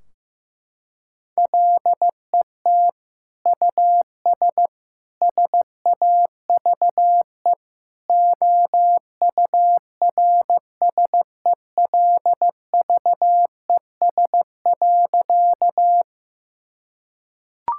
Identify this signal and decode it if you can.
Morse code, heard as a keyed tone, about 15 wpm, 700 Hz. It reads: LET US SAVE OURSELVES.
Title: Sentences from Top 1600 Words 15wpm